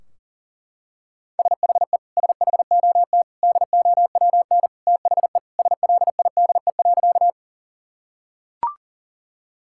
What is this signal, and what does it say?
Morse code, heard as a keyed tone, about 40 wpm, 700 Hz. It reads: SHE SHOT DOWN THE SLIDE.
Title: Sentences from Top 1600 Words 40wpm